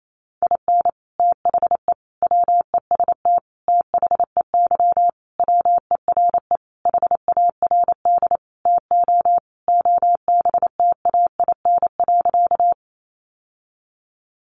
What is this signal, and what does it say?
Morse code, heard as a keyed tone, about 28 wpm, 700 Hz. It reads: IN THE WEST THEY WERE HARD TO OBTAIN.
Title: Sentences from Top 1300 Words 28wpm